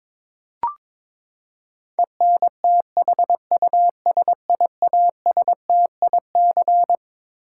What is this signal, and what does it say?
Morse code, heard as a keyed tone, about 21 wpm, 700 Hz. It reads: ENTHUSIASTIC
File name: Twelve Letter Words - 22wpm